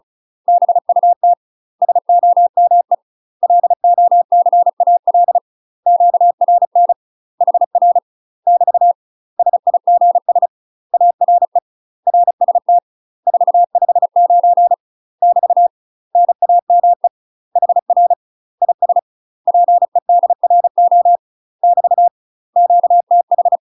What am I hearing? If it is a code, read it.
Morse code, heard as a keyed tone, about 35 wpm, 700 Hz. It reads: BUT SOME LOCAL QRN HR <BT> SIGS ARE RST 459 <BT> NAME HR IS PEDRO <BT> QTH
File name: UK MASTER QSOs TRAINING / 35WPM / UK MASTER QSOs 0003